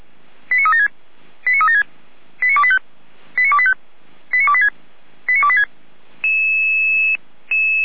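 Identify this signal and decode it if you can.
Five-tone selective call (ZVEI), heard as quick runs of five tones, tones of about 80 ms. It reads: 89376 89376 89285 89285 89286 89286
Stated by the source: ZVEI five-tone selective call received on an SDR.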